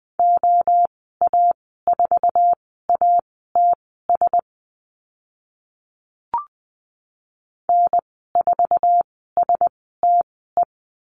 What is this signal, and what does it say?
Morse code, heard as a keyed tone, about 20 wpm, spaced as for 14 wpm, 700 Hz. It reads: OA4ATS N4STE
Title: 055 Lesson 10g - Letter r - Call Signs 20wpm